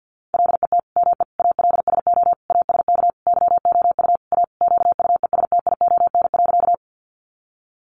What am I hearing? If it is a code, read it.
Morse code, heard as a keyed tone, about 50 wpm, 700 Hz. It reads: LET ME ALSO ASK YOU A QUESTION.
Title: Sentences from Top 400 Words 50wpm